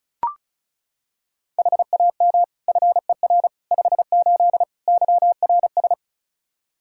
Morse code, heard as a keyed tone, about 34 wpm, 700 Hz. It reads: HAM FER 58 YRS
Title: Practice QSO - Exchange 3 - More Info - 35wpm